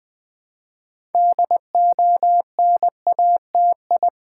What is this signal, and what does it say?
Morse code, heard as a keyed tone, about 20 wpm, 700 Hz. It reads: DONATI
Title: Build A Word - 2 Letters At A Time - 8-Letter Words - 20wpm